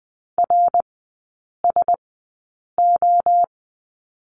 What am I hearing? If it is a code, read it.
Morse code, heard as a keyed tone, about 20 wpm, spaced as for 8 wpm, 700 Hz. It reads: RSO
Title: 053 Lesson 10e - Letter r - Three Characters 20wpm